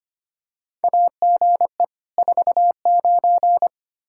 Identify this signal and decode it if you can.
Morse code, heard as a keyed tone, about 25 wpm, 700 Hz. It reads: AGE 49
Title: Practice QSO - Exchange 3 - More Info - 25wpm